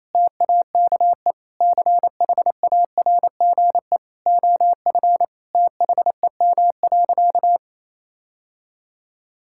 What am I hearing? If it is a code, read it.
Morse code, heard as a keyed tone, about 28 wpm, 700 Hz. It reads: TAKE CHARGE OF THEM.
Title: Sentences from Top 1000 Words - Review Part 6 - 28wpm